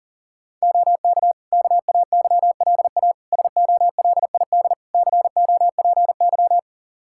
Morse code, heard as a keyed tone, about 40 wpm, 700 Hz. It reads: OK KAYLA SOLID COPY
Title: K1USN SST QSO Practice 40wpm